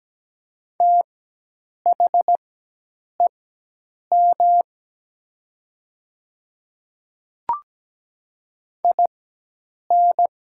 Morse code, heard as a keyed tone, about 17 wpm, spaced as for 8 wpm, 700 Hz. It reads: THEM IN
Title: 4x Character Spacing - Top 100 Words - 17wpm